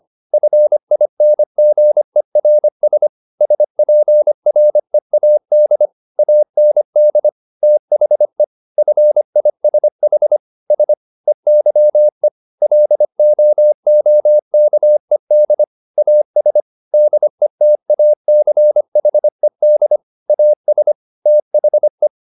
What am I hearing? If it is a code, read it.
Morse code, heard as a keyed tone, about 25 wpm, 600 Hz. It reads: FINGERS SPREAD AND THE FISH S EYE LOOKED AS DETACHED AS THE